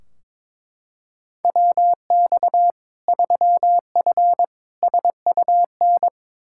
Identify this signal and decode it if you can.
Morse code, heard as a keyed tone, about 21 wpm, 700 Hz. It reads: WX 3F SUN
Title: Practice QSO - Exchange 2 - Info - 22wpm